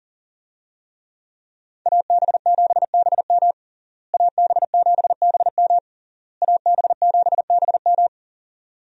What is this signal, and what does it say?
Morse code, heard as a keyed tone, about 37 wpm, 700 Hz. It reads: AB7BM AB7BM AB7BM
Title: USA Call Signs - 5 Characters - Repeated 3x - 40wpm